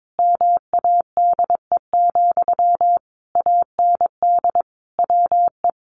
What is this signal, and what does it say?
Morse code, heard as a keyed tone, about 22 wpm, 700 Hz. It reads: MADE, AND WE
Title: Sentences from Top 1400 Words 22wpm